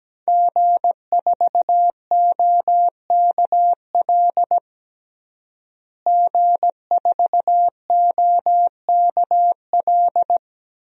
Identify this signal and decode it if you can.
Morse code, heard as a keyed tone, about 17 wpm, 700 Hz. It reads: G4OKL G4OKL
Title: DXCC-100 Call Signs - 5 Characters - Repeated 3x - 17wpm